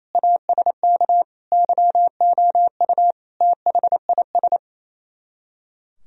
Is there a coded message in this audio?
Yes: Morse code from 0.2 to 4.6 s, heard as a keyed tone, about 35 wpm, 700 Hz. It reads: ASK YOU THIS